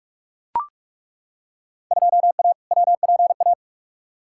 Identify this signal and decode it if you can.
Morse code, heard as a keyed tone, about 42 wpm, 700 Hz. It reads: 2A WPA
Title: Field Day Exchanges - 45wpm